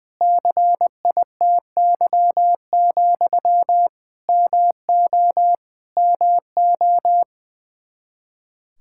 Morse code, heard as a keyed tone, about 20 wpm, 700 Hz. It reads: CITY, MO MO
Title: Practice QSO - Exchange 1 - Beginning - 20wpm